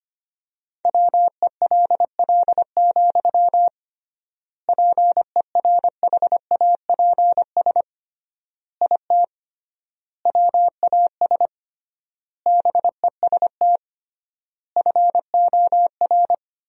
Morse code, heard as a keyed tone, about 25 wpm, 700 Hz. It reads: WELL, PERHAPS IT WAS BEST FOR
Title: 3x Word-Spacing - Sentences from Top 500 Words - 25wpm